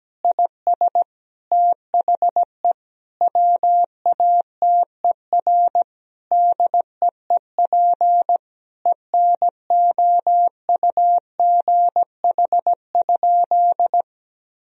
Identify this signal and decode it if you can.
Morse code, heard as a keyed tone, about 17 wpm, 700 Hz. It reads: IS THE WATER DEEP ENOUGH?
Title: Sentences from Top 1000 Words - Review Part 4 - 17wpm